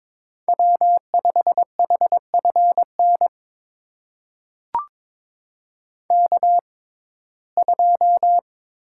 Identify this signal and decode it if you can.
Morse code, heard as a keyed tone, about 22 wpm, spaced as for 16 wpm, 700 Hz. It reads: W5HFN K2
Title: USA Call Signs - 5 Characters - 6x Character Spacing - 22wpm